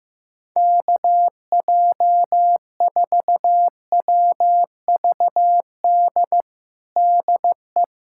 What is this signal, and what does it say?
Morse code, heard as a keyed tone, about 15 wpm, 700 Hz. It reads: KJ4WVD DE
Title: USA Callsign DE Callsign - 6 Characters - 15wpm